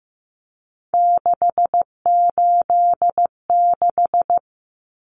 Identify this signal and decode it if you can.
Morse code, heard as a keyed tone, about 15 wpm, 700 Hz. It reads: 686
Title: Common Cut Numbers - 3-Digit 15wpm